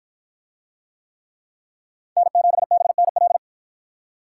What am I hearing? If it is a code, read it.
Morse code, heard as a keyed tone, about 53 wpm, 700 Hz. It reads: N7BNL